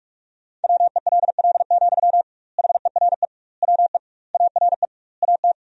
Morse code, heard as a keyed tone, about 47 wpm, 700 Hz. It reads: WELL, HERE WE ARE AT
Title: 2x Word-Spacing - Sentences from Top 300 Words - 45wpm